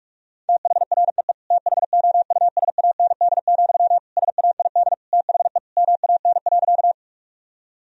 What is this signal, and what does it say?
Morse code, heard as a keyed tone, about 41 wpm, 700 Hz. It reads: THREE THOUSAND, SAID THE MAN.
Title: Sentences from Top 1000 Words - Review Part 3 - 45wpm